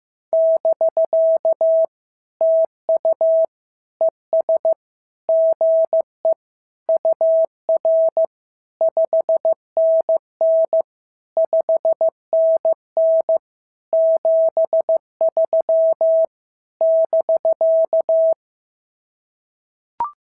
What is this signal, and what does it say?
Morse code, heard as a keyed tone, about 14 wpm, 650 Hz. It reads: <BK> TU ES GE UR 5NN 5NN 73 <BK>
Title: SOTA QSO Practice 15wpm